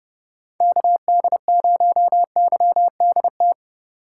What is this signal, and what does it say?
Morse code, heard as a keyed tone, about 30 wpm, 700 Hz. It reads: KD0YDT
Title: USA Call Signs - 6 Characters - 3x Character Spacing - 30wpm